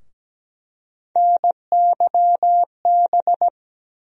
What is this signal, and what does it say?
Morse code, heard as a keyed tone, about 17 wpm, 700 Hz. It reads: NYB
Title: USA Call Sign Postfixes - 3 Letters 17wpm